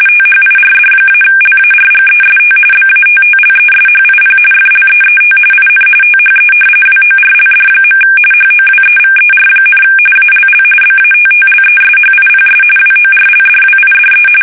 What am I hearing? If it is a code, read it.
RTTY, about 45.45 baud. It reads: THIRD-LARGEST BATTLESHIP FLEET IN THE WORLD. HOWEVER, ALL BATTLESHIPS WORLDWIDE
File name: fsk850 / battleshipnj